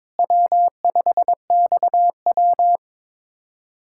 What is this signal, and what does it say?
Morse code, heard as a keyed tone, about 22 wpm, 700 Hz. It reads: W5XW
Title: USA Callsign DE Callsign - 4 Characters - 22wpm